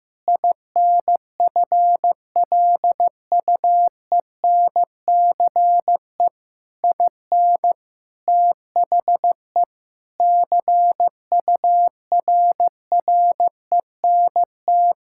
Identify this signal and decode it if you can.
Morse code, heard as a keyed tone, about 16 wpm, 700 Hz. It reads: INFLUENCE IN THE CURRENT
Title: Sentences from Top 1600 Words 15wpm